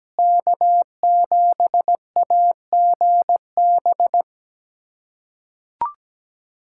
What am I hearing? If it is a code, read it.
Morse code, heard as a keyed tone, about 17 wpm, 700 Hz. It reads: K7AGB
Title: USA Callsign DE Callsign 17wpm